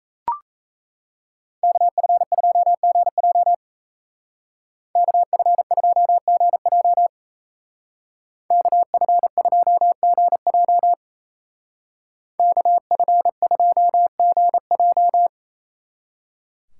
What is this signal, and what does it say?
Morse code, heard as a keyed tone, about 34 wpm, 700 Hz. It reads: KF2GJ KF2GJ KF2GJ KF2GJ
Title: Speed Racing - USA Call Signs 28wpm